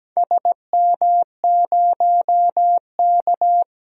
Morse code, heard as a keyed tone, about 17 wpm, 700 Hz. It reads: SM0K
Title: DXCC-100 Call Signs - 4 Characters - 17wpm